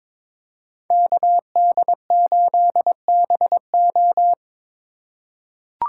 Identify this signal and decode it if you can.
Morse code, heard as a keyed tone, about 22 wpm, 700 Hz. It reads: KD8BO
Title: USA Call Signs - 5 Characters - 3x Character Spacing - 22wpm